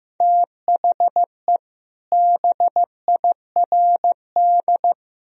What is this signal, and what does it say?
Morse code, heard as a keyed tone, about 15 wpm, 700 Hz. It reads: THE BIRD
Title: Sentences from Top 1000 Words - Review Part 2 - 15wpm